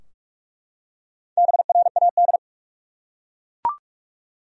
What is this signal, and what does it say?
Morse code, heard as a keyed tone, about 42 wpm, 700 Hz. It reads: BRAD
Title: USA Common Names - 45wpm